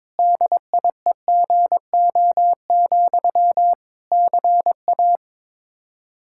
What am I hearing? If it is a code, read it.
Morse code, heard as a keyed tone, about 22 wpm, 700 Hz. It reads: DIEGO, CA
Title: Practice QSO - Exchange 1 - Beginning - 22wpm